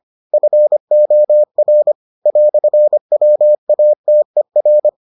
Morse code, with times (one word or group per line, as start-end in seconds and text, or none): 0.3-1.9 FOR
2.3-4.9 "WATER